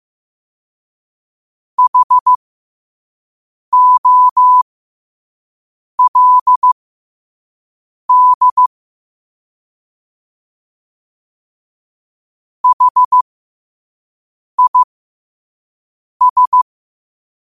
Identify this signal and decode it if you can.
Morse code, heard as a keyed tone, about 15 wpm, spaced as for 5 wpm, 1000 Hz. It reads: HOLD HIS
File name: 5wpm